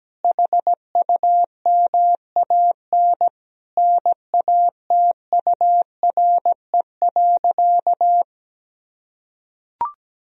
Morse code, heard as a keyed tone, about 17 wpm, 700 Hz. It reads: HUMAN NATURE.
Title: Sentences from Top 1000 Words - Review Part 3 - 17wpm